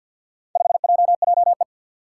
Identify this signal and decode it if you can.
Morse code, heard as a keyed tone, about 46 wpm, 700 Hz. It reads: 5JJE
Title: Four Challenging Letter-Number Combinations 50wpm